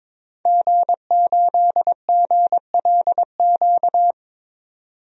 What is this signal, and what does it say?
Morse code, heard as a keyed tone, about 22 wpm, 700 Hz. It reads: G8GLQ
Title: DXCC-100 Call Signs - 5 Characters - 22wpm